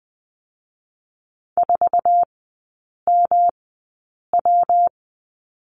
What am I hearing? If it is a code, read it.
Morse code, heard as a keyed tone, about 19 wpm, spaced as for 8 wpm, 700 Hz. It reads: 4MW
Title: 212 Lesson 34e - Number 8 - Three Characters 20wpm